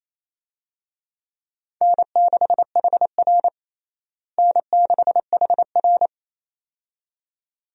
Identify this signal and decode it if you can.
Morse code, heard as a keyed tone, about 28 wpm, 700 Hz. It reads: N6HR N6HR
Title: USA Call Signs - 4 Characters - Repeated 2x - 28wpm